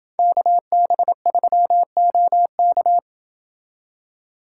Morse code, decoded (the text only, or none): KB3OK